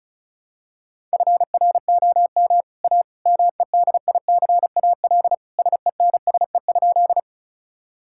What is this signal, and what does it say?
Morse code, heard as a keyed tone, about 36 wpm, 700 Hz. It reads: FROM A MEDICAL SENSE?
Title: Sentences from Top 1000 Words - Review Part 5 - 35wpm